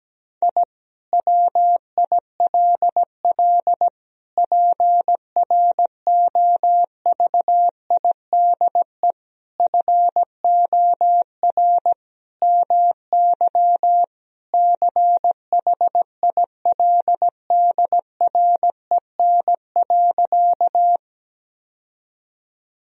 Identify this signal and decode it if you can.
Morse code, heard as a keyed tone, about 17 wpm, 700 Hz. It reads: I WILL PROVIDE FOR MY CHILDREN.
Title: Sentences from Top 1000 Words - Review Part 3 - 17wpm